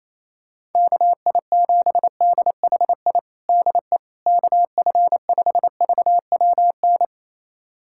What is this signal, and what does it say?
Morse code, heard as a keyed tone, about 28 wpm, 700 Hz. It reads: KI7DHI DE KF5VWN
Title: USA Callsign DE Callsign - 6 Characters - 28wpm